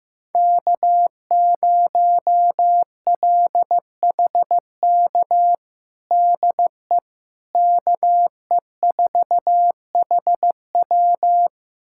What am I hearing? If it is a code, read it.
Morse code, heard as a keyed tone, about 15 wpm, 700 Hz. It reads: K0LHK DE KE4HW
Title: USA Callsign DE Callsign - 5 Characters - 15wpm